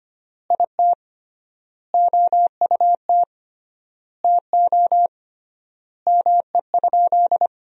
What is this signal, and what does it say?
Morse code, heard as a keyed tone, about 25 wpm, 700 Hz. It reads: IT OUT TO ME?
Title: 3x Word-Spacing - Sentences from Top 200 Words - 25wpm